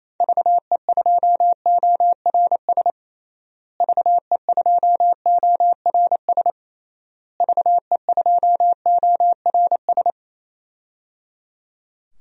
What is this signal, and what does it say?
Morse code, heard as a keyed tone, about 27 wpm, 700 Hz. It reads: VE2ORS VE2ORS VE2ORS
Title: DXCC-100 Call Signs - 6 Characters - Repeated 3x - 28wpm